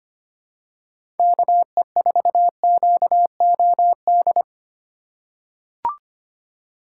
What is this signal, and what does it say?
Morse code, heard as a keyed tone, about 25 wpm, 690 Hz. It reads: KE4QOD
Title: USA Call Signs - 6 Characters - 3x Character Spacing - 25wpm